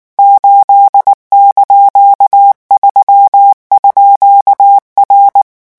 Morse code, heard as a keyed tone, about 19 wpm, 800 Hz. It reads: 8)3_R
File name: morse mp3 / revision19-vitesse 18 mots